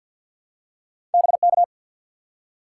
Morse code, heard as a keyed tone, about 51 wpm, 700 Hz. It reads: BK
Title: Top 90 QSO Elements 50wpm